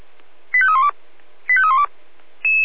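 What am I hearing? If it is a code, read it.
Five-tone selective call (ZVEI), heard as quick runs of five tones, tones of about 70 ms. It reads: 85312 85312
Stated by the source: ZVEI five-tone selective call received by SDR.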